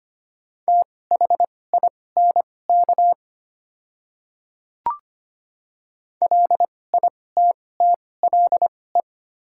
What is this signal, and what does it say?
Morse code, heard as a keyed tone, about 25 wpm, spaced as for 18 wpm, 690 Hz. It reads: THINK LITTLE
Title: 2x Character Spacing - Top 300 Words - 25wpm